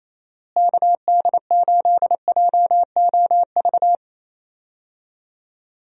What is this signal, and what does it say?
Morse code, heard as a keyed tone, about 28 wpm, 700 Hz. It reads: KD8JOV
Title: USA Call Signs - 6 Characters - Repeated 3x - 28wpm